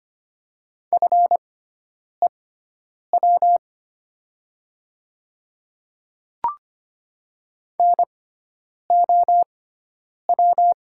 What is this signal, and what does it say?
Morse code, heard as a keyed tone, about 25 wpm, spaced as for 9 wpm, 700 Hz. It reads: FEW NOW
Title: 6x Character Spacing - Top 300 Words - 25wpm